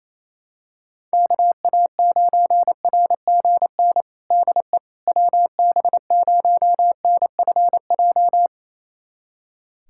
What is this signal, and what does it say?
Morse code, heard as a keyed tone, about 28 wpm, 700 Hz. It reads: KA9RGN DE WB0NFJ
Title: USA Callsign DE Callsign - 6 Characters - 28wpm